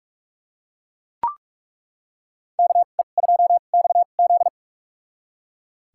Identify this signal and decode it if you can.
Morse code, heard as a keyed tone, about 41 wpm, spaced as for 31 wpm, 700 Hz. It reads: KE2XZ
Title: USA Call Signs - 5 Characters - 2x Character Spacing - 45wpm